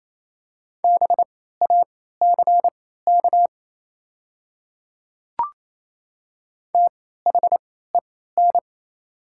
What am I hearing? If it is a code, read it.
Morse code, heard as a keyed tone, about 28 wpm, spaced as for 16 wpm, 700 Hz. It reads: BACK THEN
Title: 3x Character Spacing - Top 200 Words - 28wpm